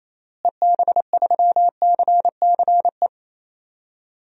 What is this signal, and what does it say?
Morse code, heard as a keyed tone, about 28 wpm, 700 Hz. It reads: EB3CCE
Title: DXCC-100 Call Signs - 6 Characters - Repeated 3x - 28wpm